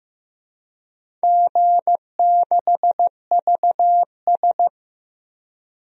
Morse code, heard as a keyed tone, about 15 wpm, 700 Hz. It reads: G6VS
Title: DXCC-100 Call Signs - 4 Characters - 15wpm